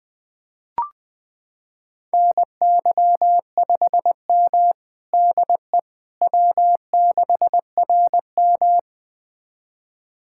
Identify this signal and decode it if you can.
Morse code, heard as a keyed tone, about 20 wpm, 700 Hz. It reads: NY5M DE W6RM
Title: USA Callsign DE Callsign - 4 Characters - 20wpm